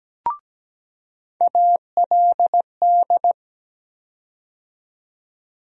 Morse code, heard as a keyed tone, about 17 wpm, 700 Hz. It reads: ALD
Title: USA Call Sign Postfixes - 3 Letters 17wpm